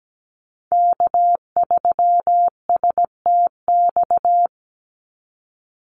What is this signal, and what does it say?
Morse code, heard as a keyed tone, about 17 wpm, 700 Hz. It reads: K3STX
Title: Call Signs Repeated 3x - 17wpm